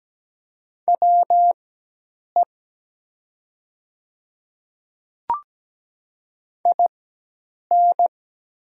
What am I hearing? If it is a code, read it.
Morse code, heard as a keyed tone, about 17 wpm, spaced as for 8 wpm, 700 Hz. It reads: WE IN
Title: 4x Character Spacing - Top 100 Words - 17wpm